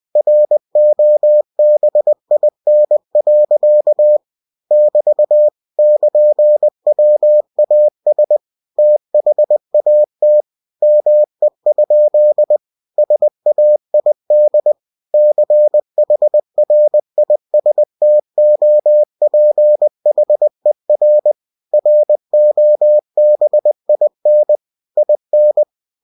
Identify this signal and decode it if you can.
Morse code, heard as a keyed tone, about 20 wpm, 600 Hz. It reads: ROBIN. <BT> <KN>WAS THAT ME? SAID CHRISTOPHER ROBIN IN
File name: cw-books / winnie-the-pooh / winnie-20wpm-page-0004